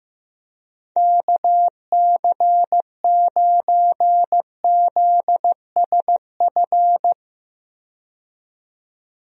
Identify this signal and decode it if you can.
Morse code, heard as a keyed tone, about 15 wpm, 700 Hz. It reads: KC9ZSF
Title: USA Call Signs - 6 Characters - 15wpm